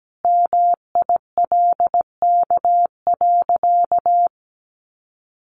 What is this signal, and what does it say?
Morse code, heard as a keyed tone, about 17 wpm, 700 Hz. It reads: MILK.
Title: Sentences from Top 1500 Words 17wpm